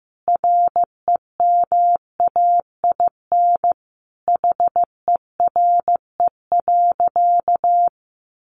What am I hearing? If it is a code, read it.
Morse code, heard as a keyed tone, about 14 wpm, 700 Hz. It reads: REMAIN HERE.
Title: Sentences from Top 1200 Words 15wpm